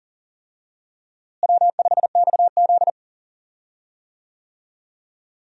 Morse code, heard as a keyed tone, about 40 wpm, 700 Hz. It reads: W5XZ
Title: USA Call Signs - 4 Characters - Repeated 2x - 40wpm